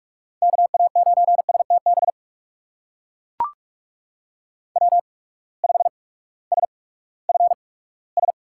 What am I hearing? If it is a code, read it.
Morse code, heard as a keyed tone, about 44 wpm, spaced as for 20 wpm, 700 Hz. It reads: KA9STB W5SFS
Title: USA Call Signs - 8x Character Spacing - 45wpm